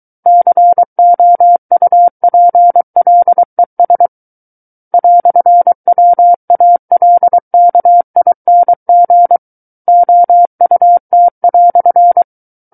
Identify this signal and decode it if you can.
Morse code, heard as a keyed tone, about 23 wpm, 700 Hz. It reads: COUPLES "WALKING OUT"
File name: Book One / 14-Chapter 14 - 23 WPM